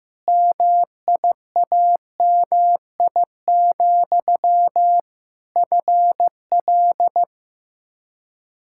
Morse code, heard as a keyed tone, about 15 wpm, 700 Hz. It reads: MIAMI, FL
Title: K1USN SST QSO Practice 15wpm